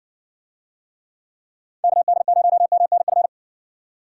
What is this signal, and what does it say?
Morse code, heard as a keyed tone, about 54 wpm, 700 Hz. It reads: KD0MNF